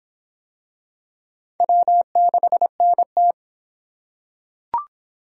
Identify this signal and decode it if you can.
Morse code, heard as a keyed tone, about 26 wpm, 700 Hz. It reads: W6NT